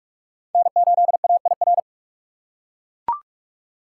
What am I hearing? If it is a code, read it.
Morse code, heard as a keyed tone, about 41 wpm, 700 Hz. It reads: N8AIR
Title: USA Call Signs - 5 Characters - 3x Character Spacing - 45wpm